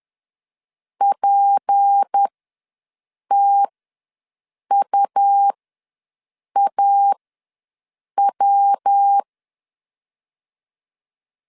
Morse code, heard as a keyed tone, about 10 wpm, spaced as for 6 wpm, 800 Hz. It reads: PTUAW
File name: LEZIONI / 6 PKX0 / 6 PKX ws 10wpm 3706